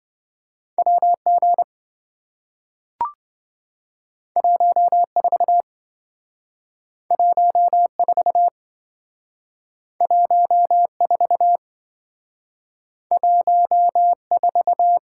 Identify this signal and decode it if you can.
Morse code, heard as a keyed tone, about 28 wpm, 700 Hz. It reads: WG 14 14 14 14